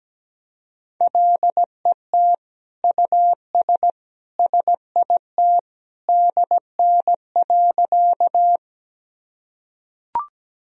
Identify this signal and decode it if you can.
Morse code, heard as a keyed tone, about 17 wpm, 700 Hz. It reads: LET US SIT DN.
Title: Sentences from Top 500 Words and Abbreviations 17wpm